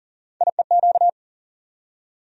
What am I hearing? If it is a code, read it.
Morse code, heard as a keyed tone, about 40 wpm, 700 Hz. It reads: IEQ
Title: USA Call Sign Postfixes - 3 Letters 40wpm